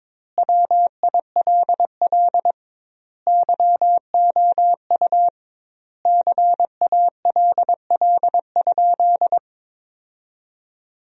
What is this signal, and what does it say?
Morse code, heard as a keyed tone, about 22 wpm, 700 Hz. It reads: WILL YOU CALL?
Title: 2x Word-Spacing - Sentences from Top 100 Words - 22wpm